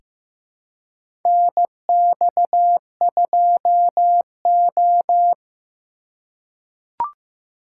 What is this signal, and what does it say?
Morse code, heard as a keyed tone, about 15 wpm, 700 Hz. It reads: NX2O
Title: USA Call Signs - 4 Characters - 4x Character Spacing - 15wpm